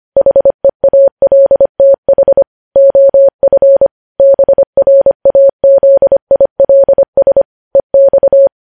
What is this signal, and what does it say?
Morse code, heard as a keyed tone, about 25 wpm, 550 Hz. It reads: HEALTH OF BRAZILS EX
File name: podcast / News headlines-1x-25x25